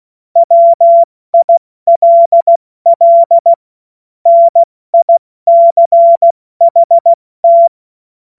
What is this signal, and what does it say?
Morse code, heard as a keyed tone, about 16 wpm, 680 Hz. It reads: WILL NICHT
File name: Morsekurs / Witze / Witz 26 März 2024 70 F80